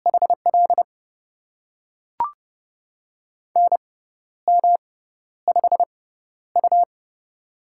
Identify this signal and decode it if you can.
Morse code, heard as a keyed tone, about 30 wpm, spaced as for 12 wpm, 700 Hz. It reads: HL NM5U
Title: USA Call Signs - 4 Characters - 6x Character Spacing - 30wpm